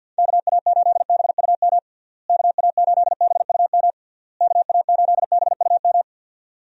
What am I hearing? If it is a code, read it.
Morse code, heard as a keyed tone, about 51 wpm, 700 Hz. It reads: KA8BUM KA8BUM KA8BUM
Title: USA Call Signs - 6 Characters - Repeated 3x - 50wpm